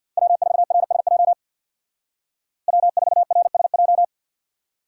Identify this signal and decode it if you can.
Morse code, heard as a keyed tone, about 52 wpm, 700 Hz. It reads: W4RSJ W4RSJ
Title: Speed Racing - USA 1x3 Call Signs 50wpm